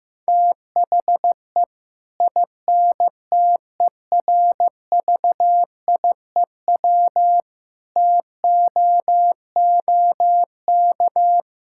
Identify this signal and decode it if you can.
Morse code, heard as a keyed tone, about 15 wpm, 700 Hz. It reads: THE INTERVIEW TOOK
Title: Sentences from Top 1600 Words 15wpm